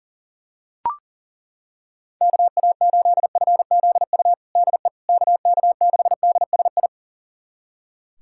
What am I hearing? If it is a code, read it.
Morse code, heard as a keyed tone, about 40 wpm, 700 Hz. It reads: KA8FZU DE KK6DSI
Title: USA Callsign DE Callsign - 6 Characters - 40wpm